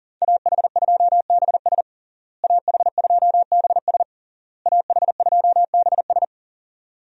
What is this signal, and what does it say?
Morse code, heard as a keyed tone, about 40 wpm, 700 Hz. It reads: AH2BS AH2BS AH2BS
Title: USA Call Signs - 5 Characters - Repeated 3x - 40wpm